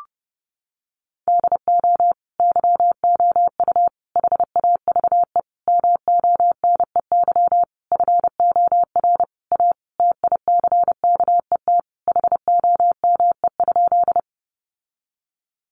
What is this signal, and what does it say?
Morse code, heard as a keyed tone, about 30 wpm, 700 Hz. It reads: DO YOU HAVE MONEY FOR A TICKET HOME?
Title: Sentences from Top 1400 Words 30wpm